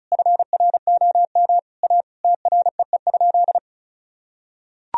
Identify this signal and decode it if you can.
Morse code, heard as a keyed tone, about 34 wpm, 700 Hz. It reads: FROM A TREE?
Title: Sentences from Top 1000 Words - Review Part 2 - 35wpm